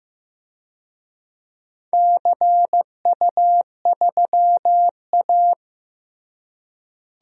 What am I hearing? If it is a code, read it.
Morse code, heard as a keyed tone, about 15 wpm, 700 Hz. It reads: CU3A
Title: DXCC-100 Call Signs - 4 Characters - Repeated 2x - 15wpm